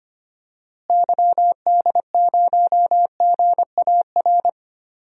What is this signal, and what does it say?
Morse code, heard as a keyed tone, about 25 wpm, 700 Hz. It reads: YD0GAR
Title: DXCC-100 Call Signs - 6 Characters - Repeated 2x - 25wpm